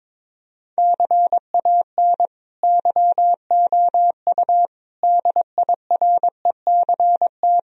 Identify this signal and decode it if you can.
Morse code, heard as a keyed tone, about 22 wpm, 700 Hz. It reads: CAN YOU DIRECT
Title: Sentences from Top 1000 Words - Review Part 2 - 22wpm